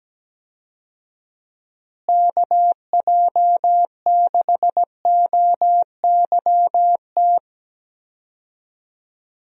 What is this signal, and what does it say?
Morse code, heard as a keyed tone, about 17 wpm, 700 Hz. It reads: KJ6OYT
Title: USA Call Signs - 6 Characters - Repeated 2x - 17wpm